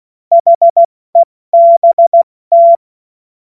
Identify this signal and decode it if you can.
Morse code, heard as a keyed tone, about 16 wpm, 680 Hz. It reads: HEBT
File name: Morsekurs / Witze / Witz 24 Juni Mai 2025 70 F80